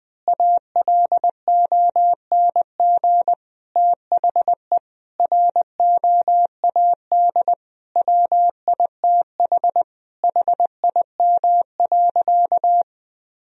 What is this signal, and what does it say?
Morse code, heard as a keyed tone, about 20 wpm, 700 Hz. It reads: ALONG THE ROAD WITH HIM.
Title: Sentences from Top 1000 Words - Review Part 2 - 20wpm